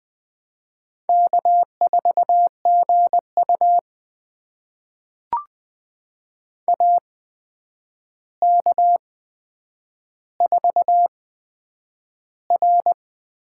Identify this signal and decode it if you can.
Morse code, heard as a keyed tone, about 20 wpm, spaced as for 9 wpm, 700 Hz. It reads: K4GU AK4R
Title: USA Call Signs - 8x Character Spacing - 20wpm